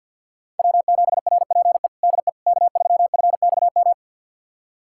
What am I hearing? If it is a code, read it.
Morse code, heard as a keyed tone, about 45 wpm, 700 Hz. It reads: W7RPE DE K3FXM
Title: USA Callsign DE Callsign - 5 Characters - 50wpm